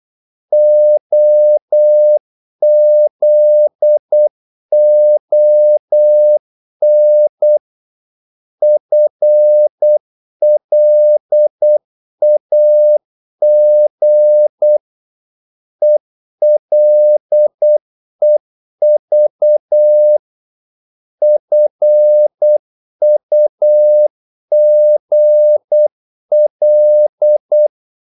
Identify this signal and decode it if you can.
Morse code, heard as a keyed tone, about 8 wpm, 600 Hz. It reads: OZON FLAG ELEV FUGL
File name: Korte ord DK 8wpm